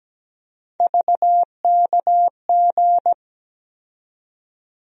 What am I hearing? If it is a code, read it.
Morse code, heard as a keyed tone, about 17 wpm, 700 Hz. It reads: VKG
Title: USA Call Sign Postfixes - 3 Letters 17wpm